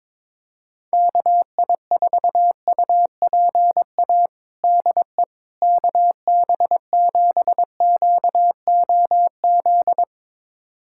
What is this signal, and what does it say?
Morse code, heard as a keyed tone, about 22 wpm, 700 Hz. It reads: KI4UPA DE KB7QOZ
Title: USA Callsign DE Callsign 22wpm